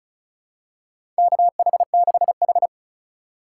Morse code, heard as a keyed tone, about 36 wpm, 700 Hz. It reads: KH6H
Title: USA Call Signs - 4 Characters - 2x Character Spacing - 35wpm